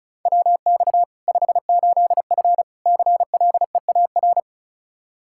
Morse code, heard as a keyed tone, about 36 wpm, 700 Hz. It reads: WX 58F CLEAR
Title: Practice QSO - Exchange 2 - Info - 35wpm